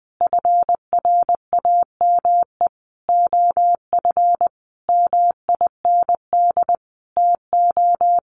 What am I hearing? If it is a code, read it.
Morse code, heard as a keyed tone, about 19 wpm, 700 Hz. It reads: FRAME OF MIND TO
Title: Sentences from Top 1400 Words 20wpm